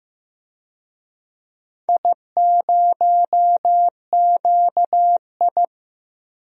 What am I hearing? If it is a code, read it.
Morse code, heard as a keyed tone, about 15 wpm, 700 Hz. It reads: I0QI
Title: DXCC-100 Call Signs - 4 Characters - 15wpm